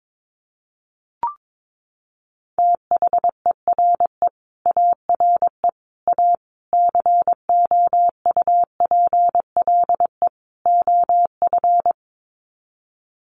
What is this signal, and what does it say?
Morse code, heard as a keyed tone, about 22 wpm, 700 Hz. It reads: THERE ARE A COUPLE OF
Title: Sets of 5 Words Encore 22wpm